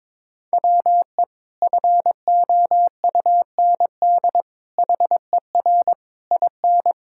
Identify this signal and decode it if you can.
Morse code, heard as a keyed tone, about 22 wpm, 700 Hz. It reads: WE FOUND HER IN
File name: practice / 22wpm / Sentences with 4-Word Phrases - 22wpm